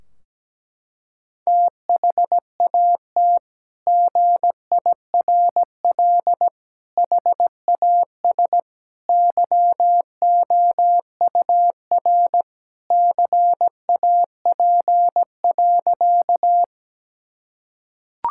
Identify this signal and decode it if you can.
Morse code, heard as a keyed tone, about 17 wpm, 700 Hz. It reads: THAT GIRL HAS YOUR CAP.
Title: Sentences from Top 1600 Words 17wpm